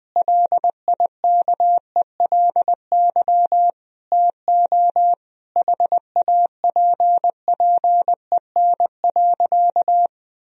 Morse code, heard as a keyed tone, about 20 wpm, 700 Hz. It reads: LIKELY TO HAPPEN.
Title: Sentences from Top 1000 Words - Review Part 4 - 20wpm